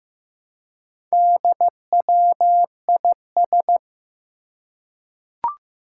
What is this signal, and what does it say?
Morse code, heard as a keyed tone, about 15 wpm, 700 Hz. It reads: DWIS
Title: Four Challenging Letter-Number Combinations 15wpm